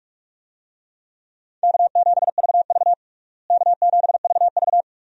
Morse code, heard as a keyed tone, about 47 wpm, 700 Hz. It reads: K7VV K7VV
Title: USA Call Signs - 4 Characters - Repeated 3x - 45wpm